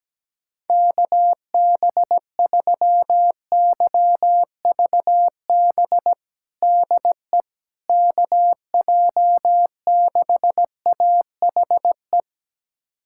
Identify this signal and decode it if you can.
Morse code, heard as a keyed tone, about 17 wpm, 700 Hz. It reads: KB3YVB DE KJ6AHE
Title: USA Callsign DE Callsign - 6 Characters - 17wpm